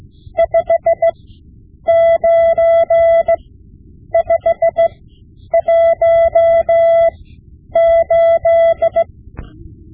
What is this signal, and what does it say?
Morse code, heard as a keyed tone, about 17 wpm, spaced as for 9 wpm, 670 Hz. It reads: 59518
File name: Morse Lessons / Morse Lessons / Tape-5B